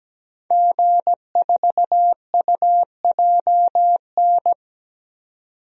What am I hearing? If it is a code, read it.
Morse code, heard as a keyed tone, about 17 wpm, 700 Hz. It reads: G4UJN
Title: DXCC-100 Call Signs - 5 Characters - 17wpm